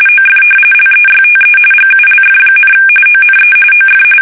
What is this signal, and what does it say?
RTTY, about 45.45 baud. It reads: OF LEYTE GULF, ONE OF
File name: fsk850